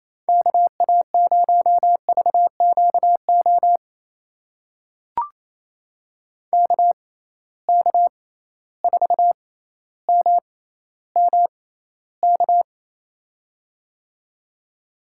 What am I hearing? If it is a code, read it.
Morse code, heard as a keyed tone, about 28 wpm, spaced as for 14 wpm, 700 Hz. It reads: KA0VQO KK4MMK
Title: USA Call Signs - 6x Character Spacing - 28wpm